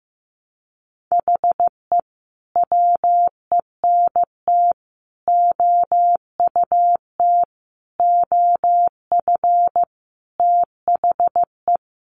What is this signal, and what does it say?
Morse code, heard as a keyed tone, about 15 wpm, 700 Hz. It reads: HE WENT OUT OF THE
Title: Sentences from Top 1100 Words 15wpm